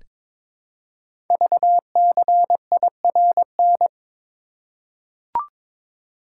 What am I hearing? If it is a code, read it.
Morse code, heard as a keyed tone, about 22 wpm, 700 Hz. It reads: VCIRN
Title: Code Groups - 5 Letter:Numbers - 22wpm